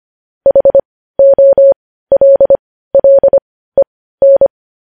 Morse code, heard as a keyed tone, about 25 wpm, spaced as for 15 wpm, 550 Hz. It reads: HOLLEN
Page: news-headlines-in-morse-code-at-15-wpm